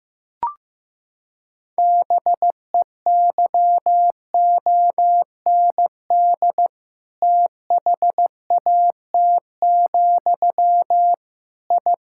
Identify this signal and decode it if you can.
Morse code, heard as a keyed tone, about 15 wpm, 700 Hz. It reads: BEYOND THAT, I
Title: Sentences from Top 1000 Words - Review Part 6 - 15wpm